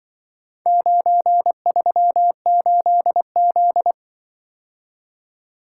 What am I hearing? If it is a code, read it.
Morse code, heard as a keyed tone, about 24 wpm, 700 Hz. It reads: 938Z
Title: Speed-Racing - Four Letter-Number Combinations - 20wpm